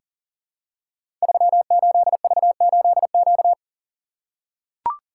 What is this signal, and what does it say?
Morse code, heard as a keyed tone, about 42 wpm, 700 Hz. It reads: 38V8Q
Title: Code Groups - 5 Letter:Numbers - 40wpm